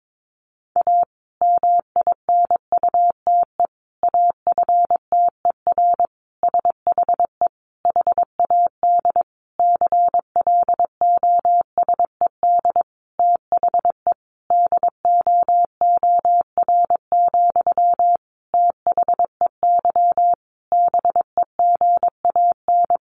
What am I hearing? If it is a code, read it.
Morse code, heard as a keyed tone, about 22 wpm, 700 Hz. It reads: A MINUTE AFTER SHE HAD CLOSED THE DOOR, THEY BEGAN
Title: Sentences from Top 1400 Words 22wpm